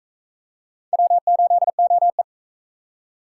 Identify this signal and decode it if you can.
Morse code, heard as a keyed tone, about 39 wpm, 700 Hz. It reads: W8OE